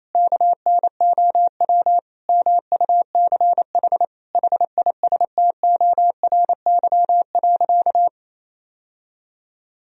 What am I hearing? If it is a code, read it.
Morse code, heard as a keyed tone, about 28 wpm, 700 Hz. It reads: KNOW MUCH HISTORY.
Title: Sentences from Top 1000 Words - Review Part 2 - 28wpm